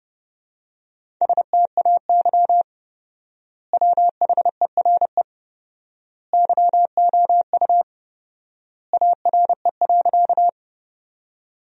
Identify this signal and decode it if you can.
Morse code, heard as a keyed tone, about 32 wpm, 700 Hz. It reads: STAY WHERE YOU ARE.
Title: 4x Word-Spacing - Sentences from Top 500 Words - 30wpm